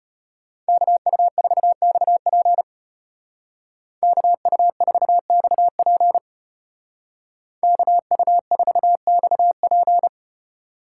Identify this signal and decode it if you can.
Morse code, heard as a keyed tone, about 30 wpm, 700 Hz. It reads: KU4XP KU4XP KU4XP
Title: Speed Racing - USA Call Signs 25wpm